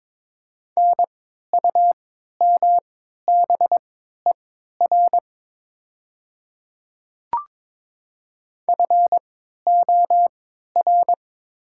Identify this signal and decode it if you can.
Morse code, heard as a keyed tone, about 22 wpm, spaced as for 13 wpm, 700 Hz. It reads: NUMBER FOR
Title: 3x Character Spacing - Top 300 Words - 22wpm